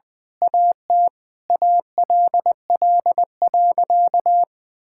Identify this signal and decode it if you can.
Morse code, heard as a keyed tone, about 19 wpm, 700 Hz. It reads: AT ALL.
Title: Sentences from Top 1600 Words 20wpm